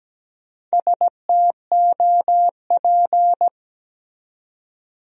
Morse code, heard as a keyed tone, about 17 wpm, 700 Hz. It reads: STOP
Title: 4x Word-Spacing - Sentences from Top 300 Words - 17wpm